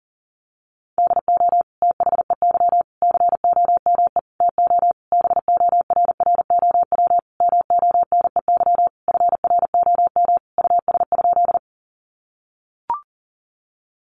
Morse code, heard as a keyed tone, about 37 wpm, 700 Hz. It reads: DO THEY COME TO BORROW MONEY FROM US?
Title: Sentences from Top 1500 Words 40wpm